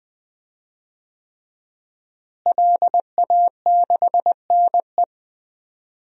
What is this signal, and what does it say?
Morse code, heard as a keyed tone, about 20 wpm, 700 Hz. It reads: LA6NE
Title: DXCC-100 Call Signs - 5 Characters - Repeated 3x - 20wpm